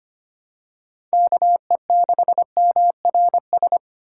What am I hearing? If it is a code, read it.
Morse code, heard as a keyed tone, about 25 wpm, 700 Hz. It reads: KE6MRS
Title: USA Call Signs - 6 Characters - 4x Character Spacing - 25wpm